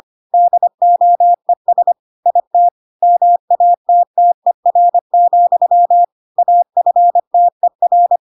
Morse code, heard as a keyed tone, about 25 wpm, 700 Hz. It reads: DOES IT MATTER, AFTER